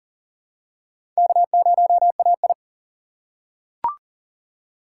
Morse code, heard as a keyed tone, about 40 wpm, 700 Hz. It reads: K0AI
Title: USA Call Signs - 4 Characters - 3x Character Spacing - 40wpm